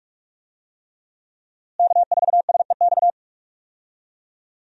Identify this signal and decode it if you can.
Morse code, heard as a keyed tone, about 46 wpm, 700 Hz. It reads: K4SEX
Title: USA Call Signs - 5 Characters - Repeated 3x - 45wpm